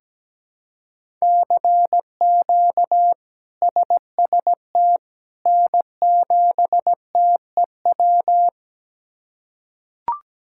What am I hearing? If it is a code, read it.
Morse code, heard as a keyed tone, about 17 wpm, 700 Hz. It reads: CQ SST N7TEW
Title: K1USN SST QSO Practice - 17wpm